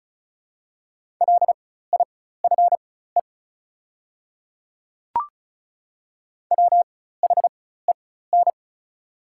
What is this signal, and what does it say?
Morse code, heard as a keyed tone, about 34 wpm, spaced as for 16 wpm, 690 Hz. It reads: LIFE WHEN
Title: 4x Character Spacing - Top 500 Words - 35wpm